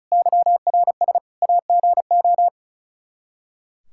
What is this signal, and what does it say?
Morse code, heard as a keyed tone, about 34 wpm, 700 Hz. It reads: YRS AGO
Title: Practice QSO - Exchange 3 - More Info - 35wpm